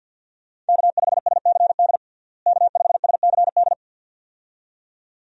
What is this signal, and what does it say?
Morse code, heard as a keyed tone, about 51 wpm, 700 Hz. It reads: K5SCD K5SCD
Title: USA Call Signs - 5 Characters - Repeated 2x - 50wpm